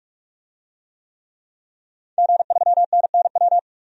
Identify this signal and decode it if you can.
Morse code, heard as a keyed tone, about 46 wpm, 700 Hz. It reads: G3NNW
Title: DXCC-100 Call Signs - 5 Characters - 45wpm